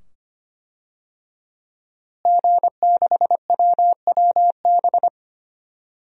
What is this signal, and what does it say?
Morse code, heard as a keyed tone, about 25 wpm, 700 Hz. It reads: G6WWB
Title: DXCC-100 Call Signs - 5 Characters - Repeated 2x - 25wpm